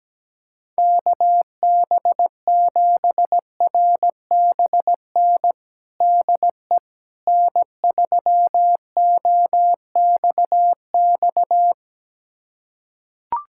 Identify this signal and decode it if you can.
Morse code, heard as a keyed tone, about 17 wpm, 700 Hz. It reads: KB7RBN DE N3OXX
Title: USA Callsign DE Callsign 17wpm